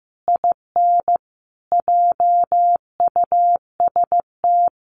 Morse code, heard as a keyed tone, about 15 wpm, 700 Hz. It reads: IN JUST
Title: Sentences from Top 300 Words 15wpm